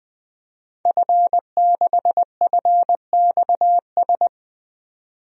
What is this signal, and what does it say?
Morse code, heard as a keyed tone, about 21 wpm, 700 Hz. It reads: F6FXS
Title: DXCC-100 Call Signs - 5 Characters - 20wpm